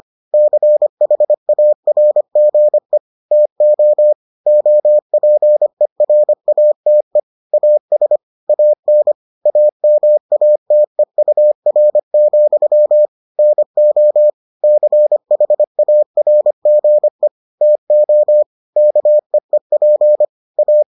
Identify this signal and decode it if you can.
Morse code, heard as a keyed tone, about 25 wpm, 600 Hz. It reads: CHARGE TO OPERATE AS AN AMATEUR, NO CHARGE TO KEEP A